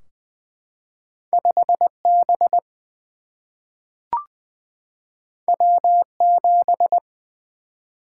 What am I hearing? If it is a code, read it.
Morse code, heard as a keyed tone, about 20 wpm, 700 Hz. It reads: 5 B W 7
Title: Single-Letter Single-Number 20wpm